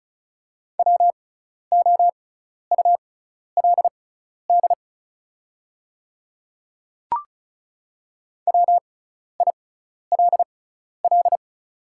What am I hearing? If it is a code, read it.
Morse code, heard as a keyed tone, about 36 wpm, spaced as for 12 wpm, 700 Hz. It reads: WOULD WILL
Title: 6x Character Spacing - Top 100 Words - 35wpm